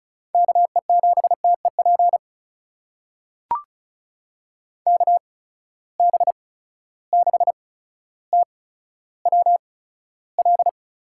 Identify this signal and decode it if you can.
Morse code, heard as a keyed tone, about 36 wpm, spaced as for 15 wpm, 700 Hz. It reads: KE7TEP KB6TWL
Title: USA Call Signs - 6 Characters - 8x Character Spacing - 35wpm